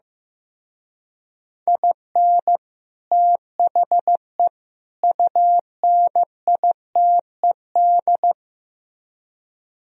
Morse code, heard as a keyed tone, about 16 wpm, 700 Hz. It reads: IN THE UNITED
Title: Speed-Racing - Sets of 3 Words 15wpm